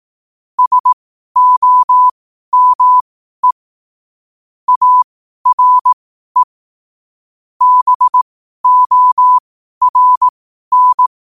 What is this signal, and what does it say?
Morse code, heard as a keyed tone, about 18 wpm, spaced as for 12 wpm, 1000 Hz. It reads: SOME ARE BORN
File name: qotdfiles / 12wpm